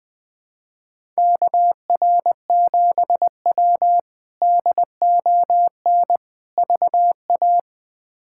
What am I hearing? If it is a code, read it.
Morse code, heard as a keyed tone, about 20 wpm, 700 Hz. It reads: KR7W DON VA
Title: K1USN SST QSO Practice - 20wpm